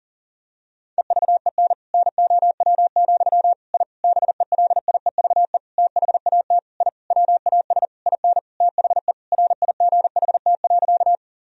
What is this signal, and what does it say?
Morse code, heard as a keyed tone, about 40 wpm, 700 Hz. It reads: EVEN NOW, I BELIEVE THAT I WAS IN THE RIGHT.
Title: Sentences from Top 1000 Words - Review Part 1 - 40wpm